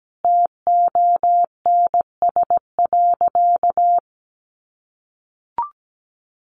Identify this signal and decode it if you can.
Morse code, heard as a keyed tone, about 17 wpm, 700 Hz. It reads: TONS.
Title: A Sound Of Thunder 17wpm